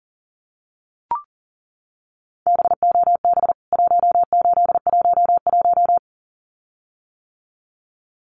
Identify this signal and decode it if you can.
Morse code, heard as a keyed tone, about 40 wpm, 700 Hz. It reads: BOB 1811
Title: CWT Exchange 40wpm